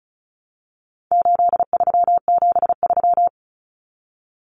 Morse code, heard as a keyed tone, about 34 wpm, 700 Hz. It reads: 8373
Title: Numbers - 4-Digit 35wpm